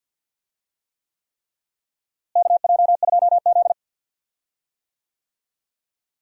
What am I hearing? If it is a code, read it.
Morse code, heard as a keyed tone, about 46 wpm, 700 Hz. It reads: KJ2Z
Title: USA Call Signs - 4 Characters - Repeated 3x - 50wpm